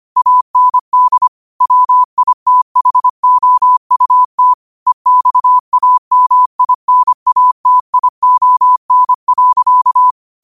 Morse code, heard as a keyed tone, about 25 wpm, 1000 Hz. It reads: AND WITHOUT EXAMINATION.